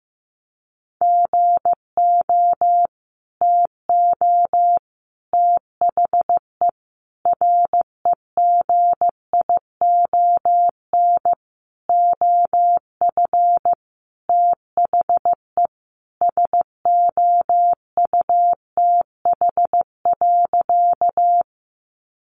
Morse code, heard as a keyed tone, about 15 wpm, 700 Hz. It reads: GO TO THE REGION OF THE SOUTH.
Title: Sentences from Top 1500 Words 15wpm